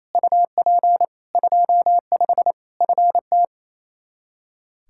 Morse code, heard as a keyed tone, about 27 wpm, 700 Hz. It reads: UP 25 FT
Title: Practice QSO - Exchange 2 - Info - 28wpm